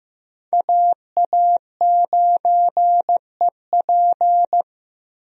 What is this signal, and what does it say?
Morse code, heard as a keyed tone, about 15 wpm, 700 Hz. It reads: AA9EP
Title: USA Call Signs - 5 Characters - 15wpm